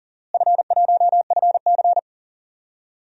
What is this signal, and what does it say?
Morse code, heard as a keyed tone, about 39 wpm, 700 Hz. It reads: F1FC
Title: DXCC-100 Call Signs - 4 Characters - Repeated 2x - 40wpm